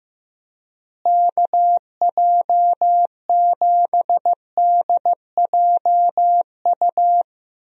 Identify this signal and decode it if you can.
Morse code, heard as a keyed tone, about 15 wpm, 700 Hz. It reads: KJ7DJU
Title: USA Call Signs - 6 Characters - Repeated 3x - 15wpm